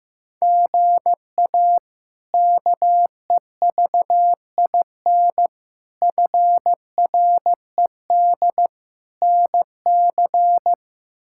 Morse code, heard as a keyed tone, about 15 wpm, 700 Hz. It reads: GA KEVIN FRED NC
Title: K1USN SST QSO Practice - 15wpm